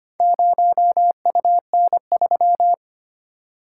Morse code, heard as a keyed tone, about 25 wpm, 700 Hz. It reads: 0UN3
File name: practice / 25wpm / Four Letter-Number Combinations - 25wpm